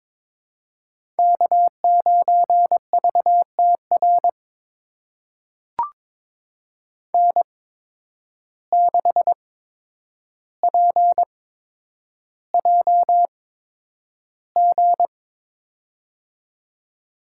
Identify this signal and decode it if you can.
Morse code, heard as a keyed tone, about 22 wpm, spaced as for 9 wpm, 700 Hz. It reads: K9VTR N6PJG
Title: USA Call Signs - 5 Characters - 8x Character Spacing - 22wpm